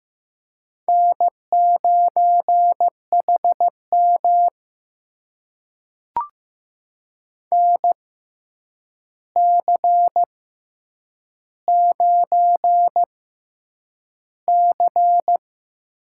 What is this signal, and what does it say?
Morse code, heard as a keyed tone, about 15 wpm, spaced as for 8 wpm, 700 Hz. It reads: N9HM NC9C
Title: USA Call Signs - 6x Character Spacing - 15wpm